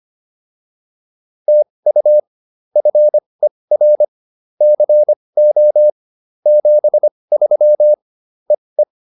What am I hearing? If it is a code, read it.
Morse code, heard as a keyed tone, about 25 wpm, spaced as for 20 wpm, 600 Hz. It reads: TU FER CO 73 EE